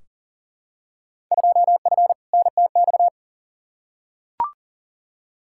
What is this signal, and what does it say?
Morse code, heard as a keyed tone, about 40 wpm, 700 Hz. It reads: 2F NTX
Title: Field Day Exchanges - 40wpm